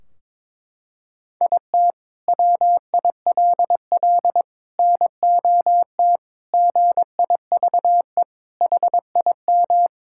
Morse code, heard as a keyed tone, about 22 wpm, 700 Hz. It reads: IT WILL NOT GIVE HIM
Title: Sentences from Top 1600 Words 22wpm